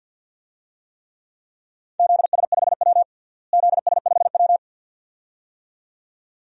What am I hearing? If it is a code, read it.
Morse code, heard as a keyed tone, about 51 wpm, 700 Hz. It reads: ZS5W ZS5W
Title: DXCC-100 Call Signs - 4 Characters - Repeated 2x - 50wpm